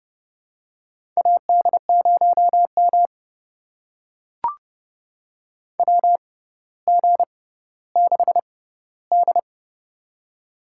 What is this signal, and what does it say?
Morse code, heard as a keyed tone, about 29 wpm, spaced as for 15 wpm, 700 Hz. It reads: AD0M WG6D
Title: USA Call Signs - 4 Characters - 6x Character Spacing - 30wpm